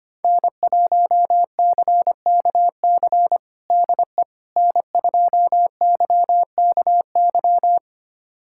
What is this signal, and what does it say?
Morse code, heard as a keyed tone, about 25 wpm, 700 Hz. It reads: N1CKC DE N2YKY
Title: USA Callsign DE Callsign - 5 Characters - 25wpm